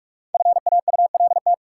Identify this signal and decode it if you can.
Morse code, heard as a keyed tone, about 44 wpm, 700 Hz. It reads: FAULT
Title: Five Letter Words - 45wpm